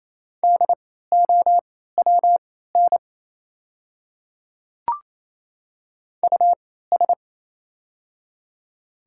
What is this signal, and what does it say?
Morse code, heard as a keyed tone, about 28 wpm, spaced as for 16 wpm, 690 Hz. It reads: DOWN US
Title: 3x Character Spacing - Top 300 Words - 28wpm